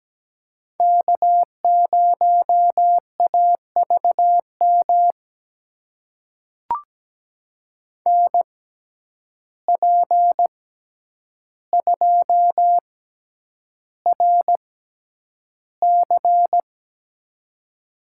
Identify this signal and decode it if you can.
Morse code, heard as a keyed tone, about 17 wpm, spaced as for 9 wpm, 700 Hz. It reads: K0AVM NP2RC
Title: USA Call Signs - 5 Characters - 6x Character Spacing - 17wpm